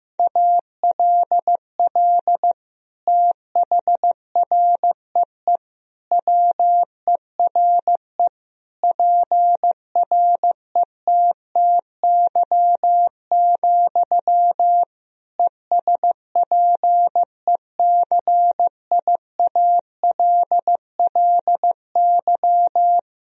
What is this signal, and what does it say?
Morse code, heard as a keyed tone, about 15 wpm, 700 Hz. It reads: ALL THREE WERE PRETTY, ESPECIALLY
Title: Sentences from Top 1000 Words - Review Part 4 - 15wpm